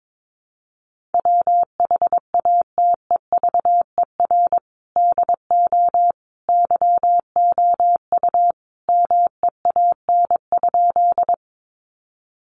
Morse code, heard as a keyed tone, about 22 wpm, 700 Hz. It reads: WHATEVER DO YOU MEAN?
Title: Sentences from Top 1100 Words 22wpm